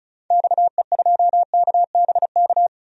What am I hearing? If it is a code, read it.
Morse code, heard as a keyed tone, about 34 wpm, 700 Hz. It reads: XE2KBK
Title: DXCC-100 Call Signs - 6 Characters - 35wpm